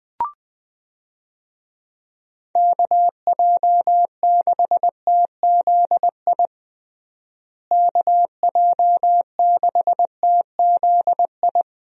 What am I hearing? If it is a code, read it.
Morse code, heard as a keyed tone, about 21 wpm, 700 Hz. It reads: KJ6TZI KJ6TZI
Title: USA Call Signs - 6 Characters - Repeated 2x - 20wpm